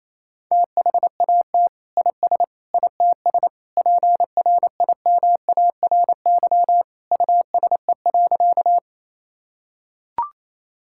Morse code, heard as a keyed tone, about 28 wpm, 700 Hz. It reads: THAT IS ITS PRIMARY USE.
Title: Sentences from Top 1000 Words - Review Part 5 - 28wpm